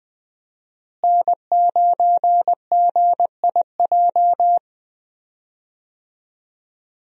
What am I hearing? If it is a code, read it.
Morse code, heard as a keyed tone, about 21 wpm, 700 Hz. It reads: N9GIJ
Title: USA Call Signs - 5 Characters - Repeated 2x - 20wpm